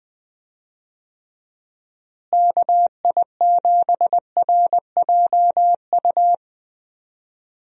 Morse code, heard as a keyed tone, about 21 wpm, 700 Hz. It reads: KI7RJU
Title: USA Call Signs - 6 Characters - Repeated 3x - 20wpm